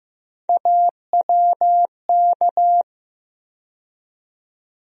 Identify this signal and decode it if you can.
Morse code, heard as a keyed tone, about 15 wpm, 700 Hz. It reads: AWK
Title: USA Call Sign Postfixes - 3 Letters 15wpm